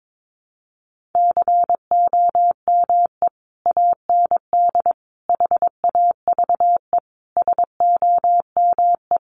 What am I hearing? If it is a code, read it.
Morse code, heard as a keyed tone, about 22 wpm, 700 Hz. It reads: COME AND HAVE SOME
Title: Sentences from Top 1500 Words 22wpm